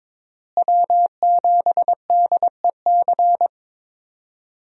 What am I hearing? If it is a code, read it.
Morse code, heard as a keyed tone, about 22 wpm, 700 Hz. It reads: W7DEC
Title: USA Call Signs - 5 Characters - Repeated 2x - 22wpm